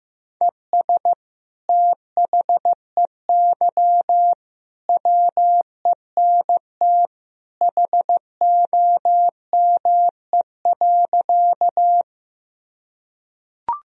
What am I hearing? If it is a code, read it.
Morse code, heard as a keyed tone, about 15 wpm, 700 Hz. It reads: ES THEY WENT HOME.
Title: Sentences from Top 200 Words and Abbreviations 15wpm